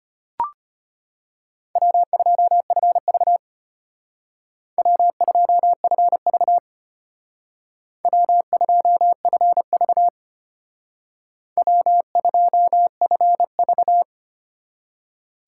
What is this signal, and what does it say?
Morse code, heard as a keyed tone, about 32 wpm, 700 Hz. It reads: W2FV W2FV W2FV W2FV